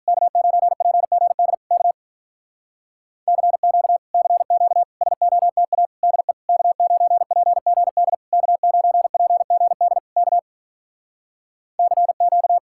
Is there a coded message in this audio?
Yes: Morse code from 0.1 to 12.6 s, heard as a keyed tone, about 48 wpm, 700 Hz. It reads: K9PGD K CQ CQ SOTA DE K9PGD K9PGD K CQ